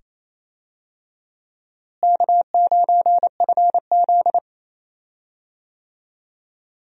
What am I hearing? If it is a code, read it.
Morse code, heard as a keyed tone, about 28 wpm, 700 Hz. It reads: K9FZ
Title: USA Call Signs - 4 Characters - Repeated 2x - 28wpm